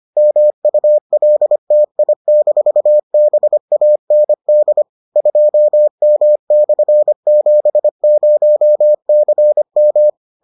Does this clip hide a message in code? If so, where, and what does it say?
Yes: Morse code from 0.2 to 10.1 s, heard as a keyed tone, about 25 wpm, 600 Hz. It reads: MULTI-BAND 2M/70CM